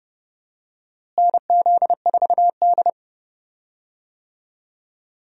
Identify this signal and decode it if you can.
Morse code, heard as a keyed tone, about 30 wpm, 700 Hz. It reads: NZ4D
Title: USA Call Signs - 4 Characters - Repeated 2x - 30wpm